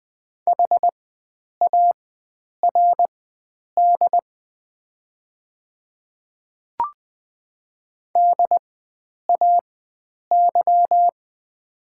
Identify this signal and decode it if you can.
Morse code, heard as a keyed tone, about 20 wpm, spaced as for 9 wpm, 690 Hz. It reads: HARD DAY
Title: 4x Character Spacing - Top 300 Words - 20wpm